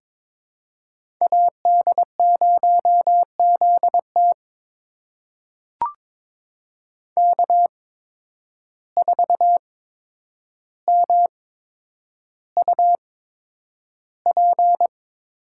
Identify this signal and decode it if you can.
Morse code, heard as a keyed tone, about 22 wpm, spaced as for 9 wpm, 700 Hz. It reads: AD0ZT K4MUP
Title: USA Call Signs - 8x Character Spacing - 22wpm